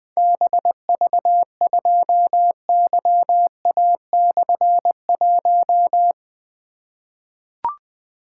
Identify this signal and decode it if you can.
Morse code, heard as a keyed tone, about 21 wpm, 700 Hz. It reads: BV2YA/1
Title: Difficult Call Signs 20wpm